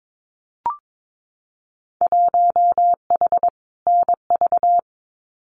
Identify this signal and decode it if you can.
Morse code, heard as a keyed tone, about 22 wpm, 700 Hz. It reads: 1H NV
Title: Winter Field Day Exchanges 22wpm